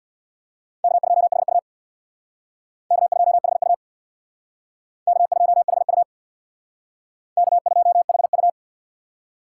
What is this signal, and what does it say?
Morse code, heard as a keyed tone, about 64 wpm, 700 Hz. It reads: K2HU K2HU K2HU K2HU